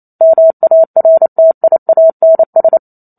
Morse code, heard as a keyed tone, about 28 wpm, 650 Hz. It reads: MARTIANS
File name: Chapter 2 - 28 WPM